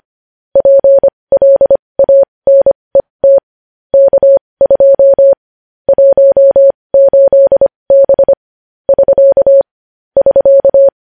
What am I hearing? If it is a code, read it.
Morse code, heard as a keyed tone, about 25 wpm, spaced as for 20 wpm, 550 Hz. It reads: PLANET K2 18B <SK> <SK>
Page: news-headlines-in-morse-code-at-20-wpm